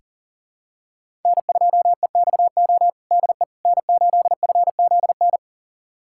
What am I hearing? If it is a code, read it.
Morse code, heard as a keyed tone, about 37 wpm, 700 Hz. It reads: N2EXO DE N8FZN
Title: USA Callsign DE Callsign - 5 Characters - 40wpm